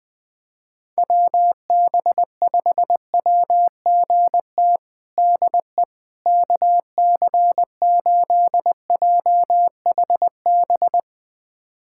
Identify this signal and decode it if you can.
Morse code, heard as a keyed tone, about 21 wpm, 700 Hz. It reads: WB5WGT DE KC8JHB
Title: USA Callsign DE Callsign 20wpm